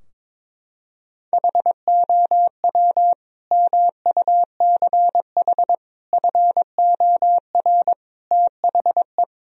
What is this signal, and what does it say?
Morse code, heard as a keyed tone, about 22 wpm, 700 Hz. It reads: HOW MUCH FOR THE
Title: Sentences from Top 1000 Words - Review Part 4 - 22wpm